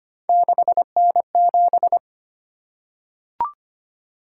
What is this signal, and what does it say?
Morse code, heard as a keyed tone, about 25 wpm, 700 Hz. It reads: 6N7
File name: Three Letter-Number Combinations - 25wpm